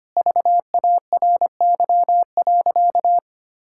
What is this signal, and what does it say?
Morse code, heard as a keyed tone, about 25 wpm, 700 Hz. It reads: VARY.
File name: practice / 25wpm / Sentences with 2-Word Phrases - 25wpm